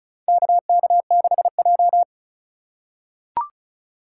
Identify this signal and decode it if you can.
Morse code, heard as a keyed tone, about 36 wpm, 700 Hz. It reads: KK6J
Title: USA Call Signs - 4 Characters - 4x Character Spacing - 35wpm